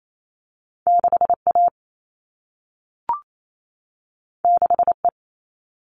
Morse code, heard as a keyed tone, about 28 wpm, 700 Hz. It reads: 6A 6E
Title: All Cut Numbers - 2-Digit 28wpm